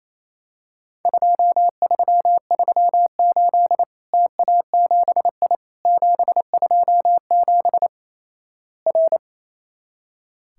Morse code, heard as a keyed tone, about 28 wpm, 700 Hz. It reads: 2338 TA7I 727 R
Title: Worked All Europe DX Contest - QTC Exchange - 28wpm